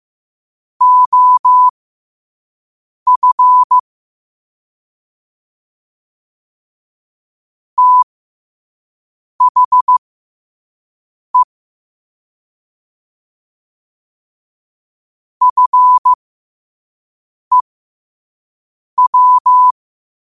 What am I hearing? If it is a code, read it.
Morse code, heard as a keyed tone, about 15 wpm, spaced as for 5 wpm, 1000 Hz. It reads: OF THE FEW
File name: qotdfiles / 5wpm